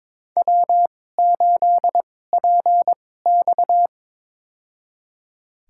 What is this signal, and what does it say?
Morse code, heard as a keyed tone, about 22 wpm, spaced as for 16 wpm, 700 Hz. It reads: W8PX
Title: USA Call Signs - 4 Characters - 2x Character Spacing - 22wpm